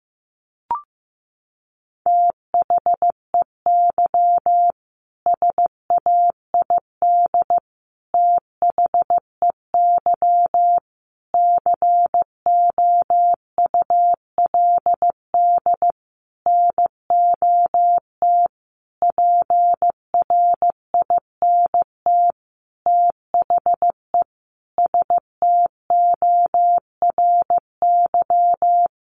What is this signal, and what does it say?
Morse code, heard as a keyed tone, about 15 wpm, 700 Hz. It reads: THEY SAID THEY COULD NOT PRINT THE STORY
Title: Sentences from Top 1400 Words 15wpm